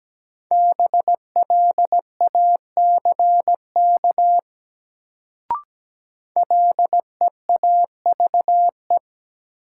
Morse code, heard as a keyed tone, about 17 wpm, 700 Hz. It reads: BLACK LEAVE
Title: Sentences from Top 400 Words - New Word Drill - 17wpm